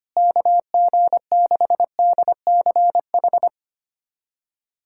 Morse code, heard as a keyed tone, about 25 wpm, 700 Hz. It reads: KG6DCH
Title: USA Callsign DE Callsign 25wpm